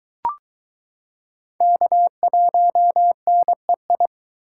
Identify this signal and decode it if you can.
Morse code, heard as a keyed tone, about 22 wpm, 700 Hz. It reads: K1NEI